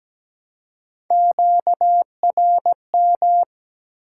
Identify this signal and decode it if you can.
Morse code, heard as a keyed tone, about 17 wpm, 700 Hz. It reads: QRM
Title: Top 90 QSO Elements 17wpm